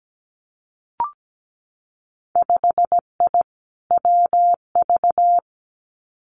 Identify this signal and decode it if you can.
Morse code, heard as a keyed tone, about 17 wpm, 700 Hz. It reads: 5I WV
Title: Winter Field Day Exchanges Encore 17wpm